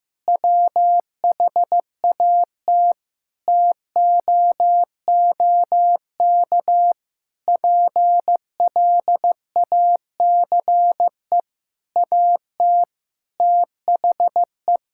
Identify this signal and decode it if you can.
Morse code, heard as a keyed tone, about 15 wpm, 700 Hz. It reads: WHAT TOOK PLACE AT THE
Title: Sentences from Top 1600 Words 15wpm